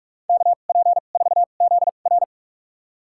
Morse code, heard as a keyed tone, about 46 wpm, spaced as for 33 wpm, 700 Hz. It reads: KP4ZR
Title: USA Call Signs - 5 Characters - 2x Character Spacing - 45wpm